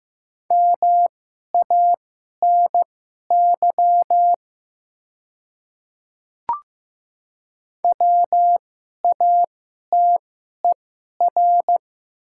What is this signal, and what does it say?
Morse code, heard as a keyed tone, about 15 wpm, spaced as for 11 wpm, 700 Hz. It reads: MANY WATER
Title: 2x Character Spacing - Top 200 Words - 15wpm